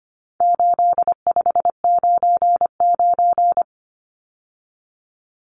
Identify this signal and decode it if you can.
Morse code, heard as a keyed tone, about 25 wpm, 700 Hz. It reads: 8599
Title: Numbers - 4-Digit 25wpm